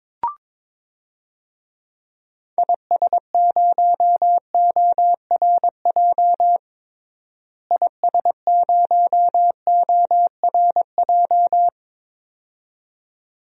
Morse code, heard as a keyed tone, about 21 wpm, 700 Hz. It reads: IS0ORJ IS0ORJ
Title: DXCC-100 Call Signs - 6 Characters - Repeated 2x - 22wpm